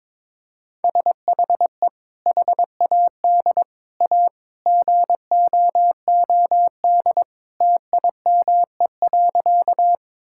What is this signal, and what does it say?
Morse code, heard as a keyed tone, about 22 wpm, 700 Hz. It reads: SHE HAD A GOOD TIME.
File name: practice / 22wpm / Sentences with 2-Word Phrases - 22wpm